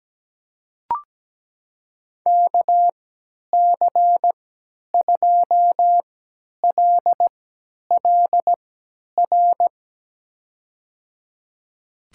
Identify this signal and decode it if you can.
Morse code, heard as a keyed tone, about 17 wpm, spaced as for 10 wpm, 700 Hz. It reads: KC2LLR
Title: USA Call Signs - 6 Characters - 3x Character Spacing - 17wpm